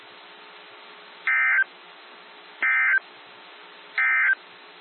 Emergency Alert System (SAME) tones, heard as harsh buzzing bursts. It is heard from 1.2 to 4.4 s.
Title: NOAA Weather Radio - WXJ73